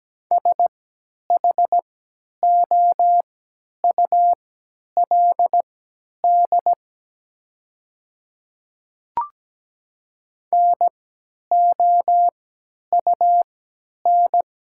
Morse code, heard as a keyed tone, about 17 wpm, spaced as for 10 wpm, 700 Hz. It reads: SHOULD NOUN
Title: 3x Character Spacing - Top 500 Words - 17wpm